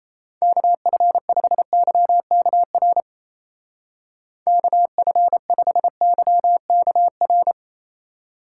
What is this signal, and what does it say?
Morse code, heard as a keyed tone, about 36 wpm, 700 Hz. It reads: KF5YKR KF5YKR